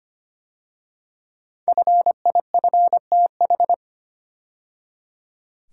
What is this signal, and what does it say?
Morse code, heard as a keyed tone, about 25 wpm, 700 Hz. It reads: FIFTH
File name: 5 Letter Words - First 3 Letters Given - 25wpm